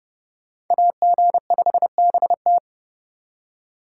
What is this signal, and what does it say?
Morse code, heard as a keyed tone, about 28 wpm, 700 Hz. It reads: AG5BT